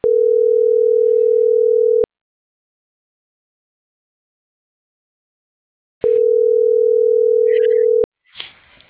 Ringback tone (North America), 440 + 480 Hz.